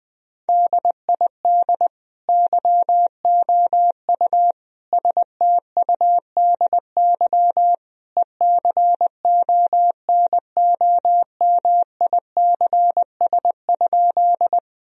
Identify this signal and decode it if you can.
Morse code, heard as a keyed tone, about 20 wpm, 700 Hz. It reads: DID YOU STUDY ECONOMICS?
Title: Sentences from Top 1000 Words - Review Part 3 - 20wpm